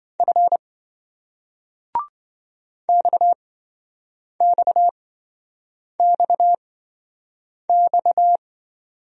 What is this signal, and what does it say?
Morse code, heard as a keyed tone, about 27 wpm, spaced as for 7 wpm, 700 Hz. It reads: F XXXX